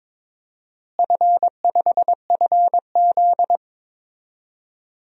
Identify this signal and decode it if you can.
Morse code, heard as a keyed tone, about 22 wpm, 700 Hz. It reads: F5FZ
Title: DXCC-100 Call Signs - 4 Characters - Repeated 2x - 22wpm